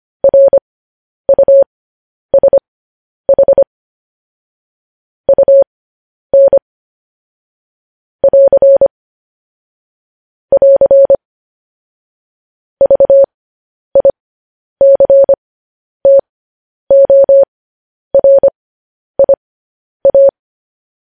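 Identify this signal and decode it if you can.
Morse code, heard as a keyed tone, about 25 wpm, spaced as for 10 wpm, 550 Hz. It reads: RUSH UN <AR> <AR> VICTORIA
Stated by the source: Morse code news headlines broadcast at 10 wpm.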